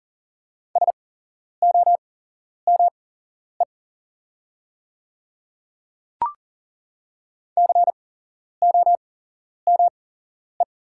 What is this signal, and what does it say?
Morse code, heard as a keyed tone, about 40 wpm, spaced as for 11 wpm, 690 Hz. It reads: SOME COME
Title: 8x Character Spacing - Top 300 Words - 40wpm